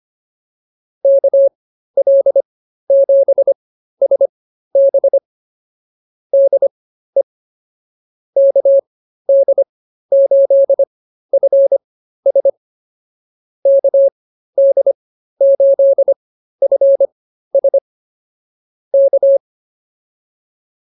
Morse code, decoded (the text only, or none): KL7SB DE KD8FS KD8FS K